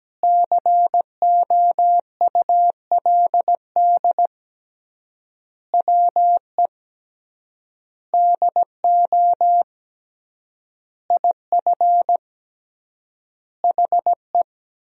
Morse code, heard as a keyed tone, about 17 wpm, 700 Hz. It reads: COULD WE DO IF HE
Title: 3x Word-Spacing - Sentences from Top 400 Words - 17wpm